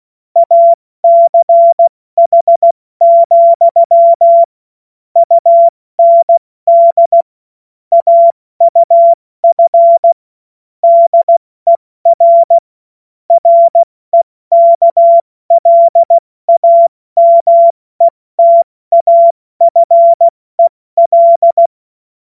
Morse code, heard as a keyed tone, about 16 wpm, 680 Hz. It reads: ACH, UND AUF DER REKLAMETAFEL
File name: Morsekurs / Witze / Witz 27. April 2021, 70 F80